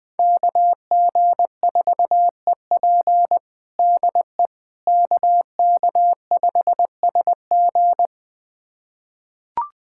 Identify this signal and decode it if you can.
Morse code, heard as a keyed tone, about 20 wpm, 700 Hz. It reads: KG4EP DE KK5SG
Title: USA Callsign DE Callsign - 5 Characters - 20wpm